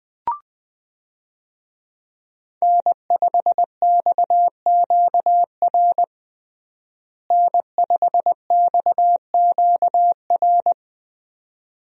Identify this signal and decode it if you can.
Morse code, heard as a keyed tone, about 20 wpm, 700 Hz. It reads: N5XQR N5XQR
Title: USA Call Signs - 5 Characters - Repeated 2x - 20wpm